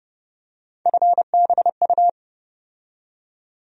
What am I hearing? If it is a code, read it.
Morse code, heard as a keyed tone, about 31 wpm, 700 Hz. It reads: FBU
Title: USA Call Sign Postfixes - 3 Letters 30wpm